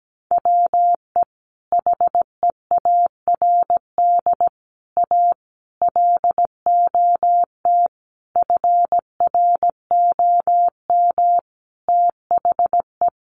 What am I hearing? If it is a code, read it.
Morse code, heard as a keyed tone, about 17 wpm, 700 Hz. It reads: WE HEARD A LOT FROM THE
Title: Sentences from Top 1100 Words 17wpm